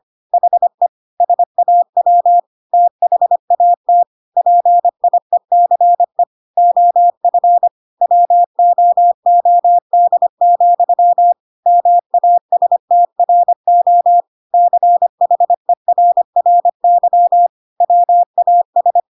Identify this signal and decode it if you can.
Morse code, heard as a keyed tone, about 25 wpm, 700 Hz. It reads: HE SAW THAT PIECE OF WOOD, MASTRO CHERRY WAS
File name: Pinocchio / Pinocchio ch1 25 25 ik1hge